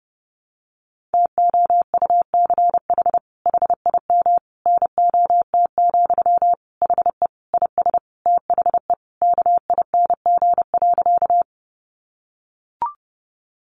Morse code, heard as a keyed tone, about 30 wpm, 700 Hz. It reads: TOUCH HIM NOT, HE IS THE KING.
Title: Sentences from Top 1200 Words 30wpm